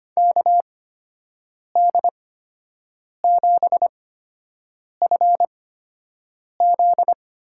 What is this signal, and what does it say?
Morse code, heard as a keyed tone, about 25 wpm, spaced as for 7 wpm, 700 Hz. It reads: KD7FZ
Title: USA Call Signs - 8x Character Spacing - 25wpm